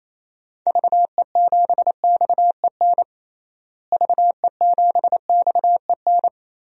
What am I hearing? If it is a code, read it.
Morse code, heard as a keyed tone, about 28 wpm, 700 Hz. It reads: VE7XEN VE7XEN
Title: DXCC-100 Call Signs - 6 Characters - Repeated 2x - 28wpm